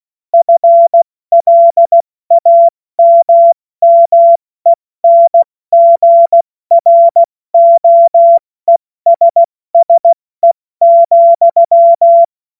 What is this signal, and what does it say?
Morse code, heard as a keyed tone, about 16 wpm, 680 Hz. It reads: FLAMMENGROESSE,
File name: Morsekurs / Witze / Witz 14. Juni 2022 70 F80